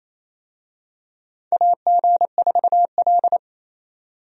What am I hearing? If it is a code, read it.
Morse code, heard as a keyed tone, about 28 wpm, 700 Hz. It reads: AG4L
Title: USA Call Signs - 4 Characters - 28wpm